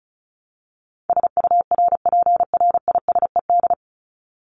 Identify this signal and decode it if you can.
Morse code, heard as a keyed tone, about 34 wpm, 700 Hz. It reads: SURPRISED
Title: Top 1501-1750 Words 35wpm